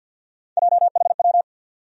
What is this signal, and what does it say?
Morse code, heard as a keyed tone, about 47 wpm, 700 Hz. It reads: JHW
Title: USA Call Sign Postfixes - 3 Letters 50wpm